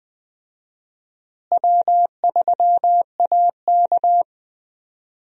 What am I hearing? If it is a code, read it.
Morse code, heard as a keyed tone, about 20 wpm, 700 Hz. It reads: W3AK
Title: USA Call Signs - 4 Characters - Repeated 2x - 20wpm